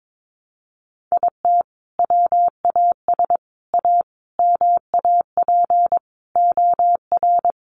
Morse code, heard as a keyed tone, about 22 wpm, 700 Hz. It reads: IT WAS A MAP OR
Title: Sentences from Top 1300 Words 22wpm